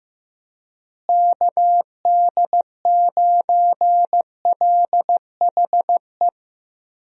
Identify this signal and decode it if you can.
Morse code, heard as a keyed tone, about 16 wpm, 700 Hz. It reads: KD9LHE
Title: USA Call Signs - 6 Characters - Repeated 2x - 15wpm